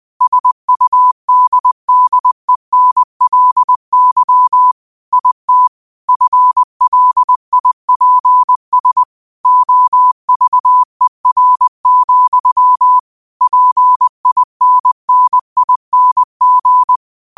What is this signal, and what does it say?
Morse code, heard as a keyed tone, about 20 wpm, 1000 Hz. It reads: SUDDENLY IT FLIPS OVER, PINNING
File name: qotdfiles / 20wpm